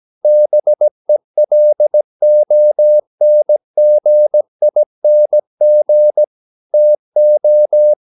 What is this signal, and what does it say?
Morse code, heard as a keyed tone, about 17 wpm, 600 Hz. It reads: BELONGING TO
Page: Romanen Never af Ken Follet på engelsk.